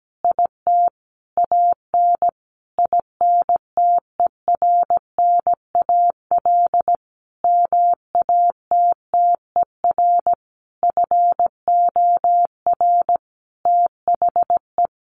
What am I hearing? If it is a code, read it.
Morse code, heard as a keyed tone, about 17 wpm, 700 Hz. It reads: IT AN INTERNAL MATTER FOR THE
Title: Sentences from Top 1400 Words 17wpm